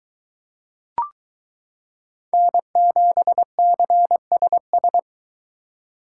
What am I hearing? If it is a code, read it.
Morse code, heard as a keyed tone, about 22 wpm, 700 Hz. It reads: N7CSS